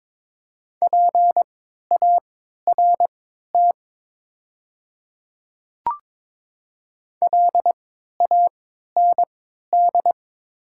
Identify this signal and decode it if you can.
Morse code, heard as a keyed tone, about 22 wpm, spaced as for 12 wpm, 700 Hz. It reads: PART LAND
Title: 3x Character Spacing - Top 200 Words - 22wpm